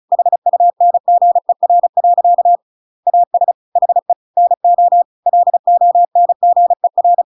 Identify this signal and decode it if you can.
Morse code, heard as a keyed tone, about 34 wpm, 700 Hz. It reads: HUNGER. AS HE NO LONGER